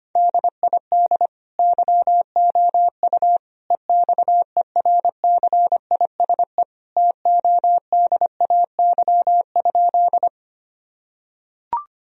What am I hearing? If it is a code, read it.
Morse code, heard as a keyed tone, about 25 wpm, 690 Hz. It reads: DID YOU EXERCISE TODAY?
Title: Sentences from Top 1000 Words - Review Part 4 - 25wpm